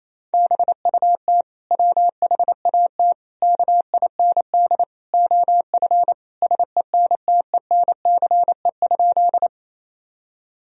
Morse code, heard as a keyed tone, about 28 wpm, 700 Hz. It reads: BUT WHAT KIND OF SENTENCE?
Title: Sentences from Top 1000 Words - Review Part 1 - 28wpm